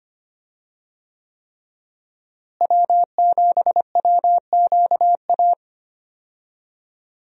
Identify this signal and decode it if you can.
Morse code, heard as a keyed tone, about 25 wpm, 700 Hz. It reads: W7WQA
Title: USA Call Signs - 5 Characters - Repeated 3x - 25wpm